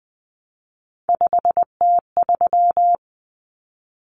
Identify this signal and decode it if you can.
Morse code, heard as a keyed tone, about 20 wpm, 700 Hz. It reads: 5T3
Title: Common Cut Numbers - 3-Digit 20wpm